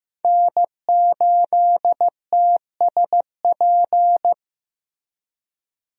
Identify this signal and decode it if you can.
Morse code, heard as a keyed tone, about 15 wpm, 700 Hz. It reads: N8TSP
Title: USA Call Signs - 5 Characters - Repeated 2x - 15wpm